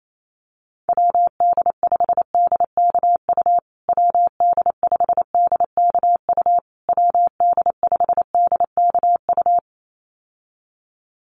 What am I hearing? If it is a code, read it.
Morse code, heard as a keyed tone, about 28 wpm, 700 Hz. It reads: WD5DKU WD5DKU WD5DKU
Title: USA Call Signs Repeated 3x 28wpm